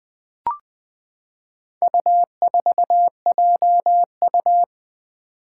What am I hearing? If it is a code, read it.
Morse code, heard as a keyed tone, about 20 wpm, 700 Hz. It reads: U4JU
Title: Four Challenging Letter-Number Combinations 20wpm